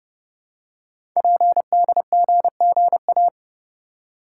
Morse code, heard as a keyed tone, about 30 wpm, 700 Hz. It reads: PDGGA
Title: Code Groups - 5 Letters - 30wpm